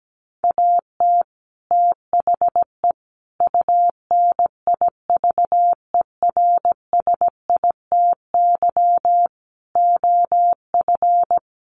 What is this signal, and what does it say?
Morse code, heard as a keyed tone, about 17 wpm, 700 Hz. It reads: AT THE UNIVERSITY OF
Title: Sets of 5 Words Encore2 17wpm